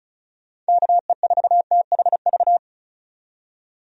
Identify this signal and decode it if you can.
Morse code, heard as a keyed tone, about 34 wpm, 700 Hz. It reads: KE4THV
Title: USA Call Signs - 6 Characters - 35wpm